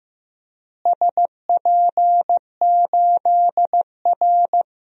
Morse code, heard as a keyed tone, about 15 wpm, 700 Hz. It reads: SP8R
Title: DXCC-100 Call Signs - 4 Characters - Repeated 3x - 15wpm